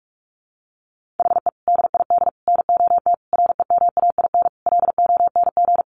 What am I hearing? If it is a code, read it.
Morse code, heard as a keyed tone, about 44 wpm, 700 Hz. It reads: HE DID NOT REMAIN LONG
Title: Sentences from Top 1500 Words 45wpm